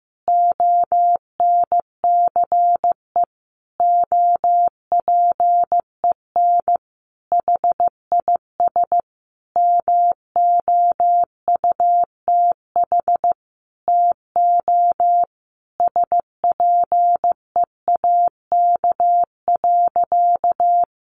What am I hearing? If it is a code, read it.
Morse code, heard as a keyed tone, about 15 wpm, 700 Hz. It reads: ONCE OPEN HIS MOUTH TO SPEAK.
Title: Sentences from Top 1500 Words 15wpm